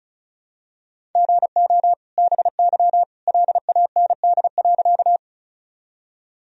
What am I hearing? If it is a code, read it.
Morse code, heard as a keyed tone, about 34 wpm, 700 Hz. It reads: GO BY LAND.
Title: Sentences from Top 1000 Words - Review Part 1 - 35wpm